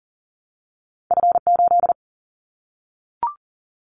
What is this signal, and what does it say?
Morse code, heard as a keyed tone, about 40 wpm, 700 Hz. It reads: F8
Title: Callsign Prefixes for Top 25 DXCC - 40wpm